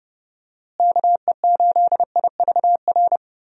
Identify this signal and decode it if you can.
Morse code, heard as a keyed tone, about 30 wpm, 700 Hz. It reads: KE8IVR
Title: USA Call Signs - 6 Characters - Repeated 2x - 30wpm